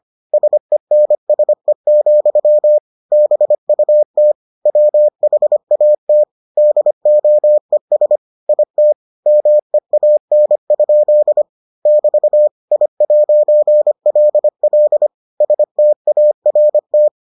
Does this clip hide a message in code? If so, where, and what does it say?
Yes: Morse code from 0.3 to 17.1 s, heard as a keyed tone, about 25 wpm, 600 Hz. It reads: SENSE, BUT WHAT DOES IT MEAN? <BT> I'LL START